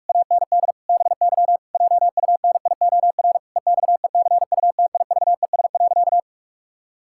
Morse code, heard as a keyed tone, about 46 wpm, 700 Hz. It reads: AND BY JUNIOR EXECUTIVES.
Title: Pangram Challenge v2 - 45wpm